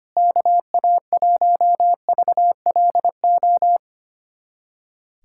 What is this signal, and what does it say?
Morse code, heard as a keyed tone, about 25 wpm, 700 Hz. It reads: KA1VLO
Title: USA Callsign DE Callsign - 6 Characters - 25wpm